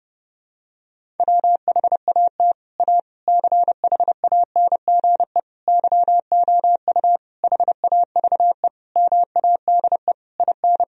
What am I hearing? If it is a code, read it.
Morse code, heard as a keyed tone, about 32 wpm, 700 Hz. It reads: WHAT A CHANGE YOU HAVE MADE IN
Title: Sentences from Top 1000 Words - Review Part 1 - 30wpm